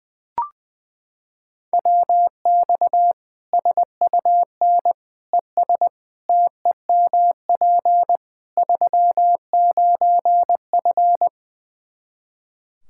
Morse code, heard as a keyed tone, about 20 wpm, 700 Hz. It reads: WX SUN ES TEMP 39F
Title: Practice QSO - Exchange 2 - Info - 20wpm